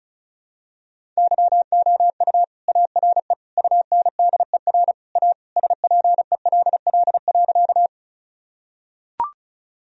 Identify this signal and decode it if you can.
Morse code, heard as a keyed tone, about 36 wpm, 700 Hz. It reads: YOU ARE UNDER A SPELL.
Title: Sentences from Top 1000 Words - Review Part 1 - 35wpm